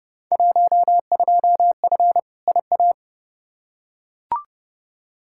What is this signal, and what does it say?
Morse code, heard as a keyed tone, about 32 wpm, 700 Hz. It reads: 12F IA
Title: Field Day Exchanges Encore - 30wpm